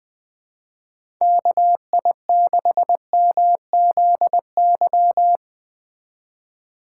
Morse code, heard as a keyed tone, about 21 wpm, 700 Hz. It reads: KI6MZY
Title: USA Call Signs - 2x Character Spacing - 20wpm